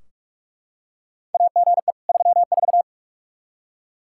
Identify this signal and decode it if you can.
Morse code, heard as a keyed tone, about 44 wpm, 700 Hz. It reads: AGE 34
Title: Practice QSO - Exchange 3 - More Info - 45wpm